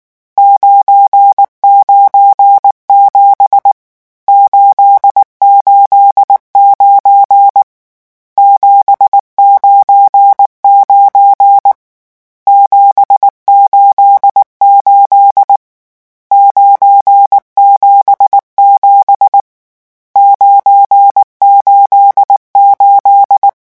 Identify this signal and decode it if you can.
Morse code, heard as a keyed tone, about 19 wpm, 800 Hz. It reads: 997 889 799 788 977 988
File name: morse mp3 / lecon12-vitesse 18